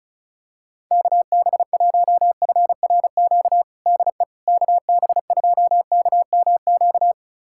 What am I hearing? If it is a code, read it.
Morse code, heard as a keyed tone, about 34 wpm, 700 Hz. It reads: KB1FRQ DE KB2KMQ
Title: USA Callsign DE Callsign - 6 Characters - 35wpm